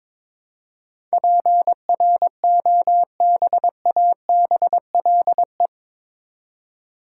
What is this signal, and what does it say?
Morse code, heard as a keyed tone, about 22 wpm, 700 Hz. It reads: PROBABLE
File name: practice / 22wpm / Eight Letter Words - 22wpm